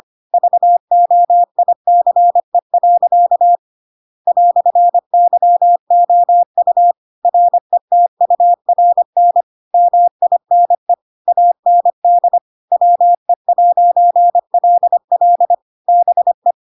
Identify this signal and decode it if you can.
Morse code, heard as a keyed tone, about 25 wpm, 700 Hz. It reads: VOICE. "YOU RETURN MINE AND WE'LL BE
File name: Pinocchio / Pinocchio ch2 25 25 ik1hge